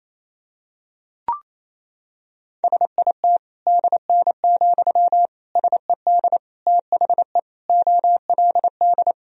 Morse code, heard as a keyed tone, about 28 wpm, 700 Hz. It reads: SIT DN, SED THE OLD
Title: Sentences from Top 500 Words and Abbreviations 28wpm